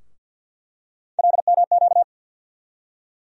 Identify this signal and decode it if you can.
Morse code, heard as a keyed tone, about 47 wpm, 700 Hz. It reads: LMQ
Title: USA Call Sign Postfixes - 3 Letters 50wpm